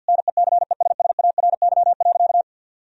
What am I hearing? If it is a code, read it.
Morse code, heard as a keyed tone, about 51 wpm, 700 Hz. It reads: NECESSARY.
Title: Sentences from Top 1000 Words - Review Part 3 - 50wpm